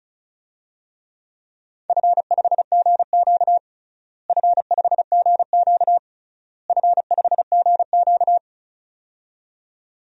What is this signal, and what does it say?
Morse code, heard as a keyed tone, about 36 wpm, 700 Hz. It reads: F5GQ F5GQ F5GQ
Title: DXCC-100 Call Signs - 4 Characters - Repeated 3x - 35wpm